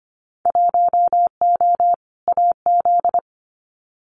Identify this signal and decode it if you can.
Morse code, heard as a keyed tone, about 25 wpm, 700 Hz. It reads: 1O AZ
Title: Winter Field Day Exchanges 25wpm